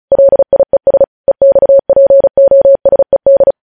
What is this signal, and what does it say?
Morse code, heard as a keyed tone, about 36 wpm, 550 Hz. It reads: LIES EXPOSED